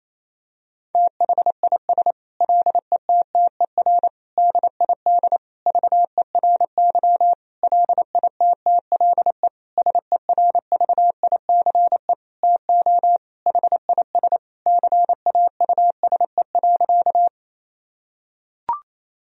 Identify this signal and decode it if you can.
Morse code, heard as a keyed tone, about 28 wpm, 700 Hz. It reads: THIS LETTER DID VERY LITTLE SERVICE TO HIS CAUSE.
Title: Sentences from Top 1000 Words - Review Part 4 - 28wpm